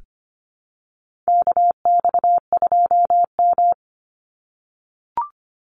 Morse code, heard as a keyed tone, about 25 wpm, 700 Hz. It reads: KX2M
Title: USA Call Signs 25wpm